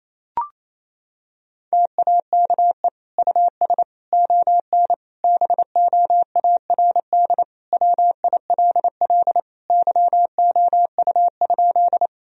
Morse code, heard as a keyed tone, about 28 wpm, 700 Hz. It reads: TAKE US ON BOARD WILL YOU?
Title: Sentences from Top 1000 Words - Review Part 4 - 28wpm